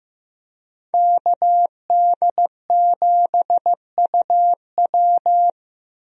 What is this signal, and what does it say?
Morse code, heard as a keyed tone, about 15 wpm, 700 Hz. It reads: KD7UW
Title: USA Call Signs - 5 Characters - 15wpm